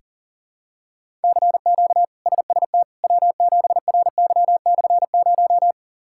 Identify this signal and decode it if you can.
Morse code, heard as a keyed tone, about 40 wpm, 700 Hz. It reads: CQ SST W7RY/0
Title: K1USN SST QSO Practice - 40wpm